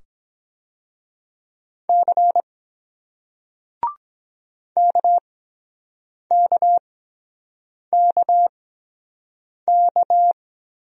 Morse code, heard as a keyed tone, about 22 wpm, spaced as for 7 wpm, 700 Hz. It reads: C KKKK